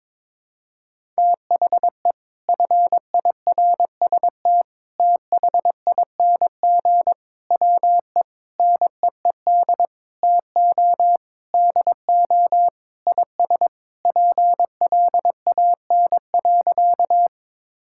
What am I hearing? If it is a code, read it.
Morse code, heard as a keyed tone, about 22 wpm, 700 Hz. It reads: THE FIRST THING WE NEED TO DO IS PLAN.
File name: practice / 22wpm / Sentences with 8-Word Phrases - 22wpm